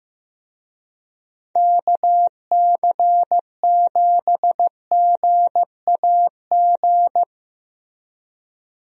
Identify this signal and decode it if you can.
Morse code, heard as a keyed tone, about 15 wpm, 700 Hz. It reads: KC7GAG
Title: USA Call Signs - 6 Characters - Repeated 2x - 15wpm